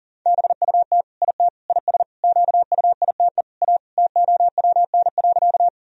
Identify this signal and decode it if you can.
Morse code, heard as a keyed tone, about 40 wpm, 700 Hz. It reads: BUT IT IS QUITE A TOWN.
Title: Sentences from Top 1000 Words - Review Part 2 - 40wpm